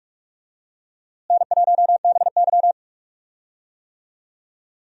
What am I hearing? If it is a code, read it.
Morse code, heard as a keyed tone, about 46 wpm, 700 Hz. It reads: N1BY
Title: USA Call Signs - 4 Characters - Repeated 2x - 45wpm